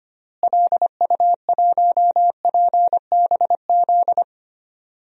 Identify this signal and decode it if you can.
Morse code, heard as a keyed tone, about 25 wpm, 700 Hz. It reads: LU1PBZ
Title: DXCC-100 Call Signs - 6 Characters - Repeated 2x - 25wpm